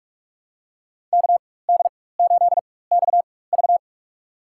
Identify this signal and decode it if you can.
Morse code, heard as a keyed tone, about 41 wpm, spaced as for 20 wpm, 700 Hz. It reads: KD8XV
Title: USA Call Signs - 5 Characters - 4x Character Spacing - 45wpm